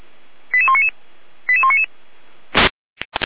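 Five-tone selective call (ZVEI), heard as quick runs of five tones, tones of about 70 ms. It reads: 88199 88199
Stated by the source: ZVEI five-tone call 89042 received by SDR on channel 455.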